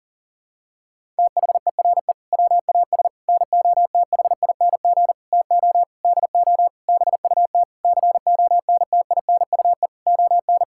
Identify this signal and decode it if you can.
Morse code, heard as a keyed tone, about 40 wpm, 700 Hz. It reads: THERE WAS NOTHING TO DO BUT CONTINUE ON
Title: Sentences from Top 1000 Words - Review Part 6 - 40wpm